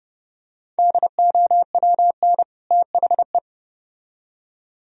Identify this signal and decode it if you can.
Morse code, heard as a keyed tone, about 28 wpm, 700 Hz. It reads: DOWN THE